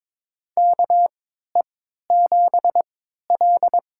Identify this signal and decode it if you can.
Morse code, heard as a keyed tone, about 22 wpm, spaced as for 13 wpm, 700 Hz. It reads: KE7L
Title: USA Call Signs - 3x Character Spacing - 22wpm